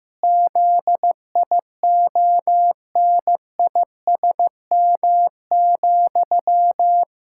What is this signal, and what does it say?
Morse code, heard as a keyed tone, about 15 wpm, 700 Hz. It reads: ZIONISM,
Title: Pangram Challenge v2 - 15wpm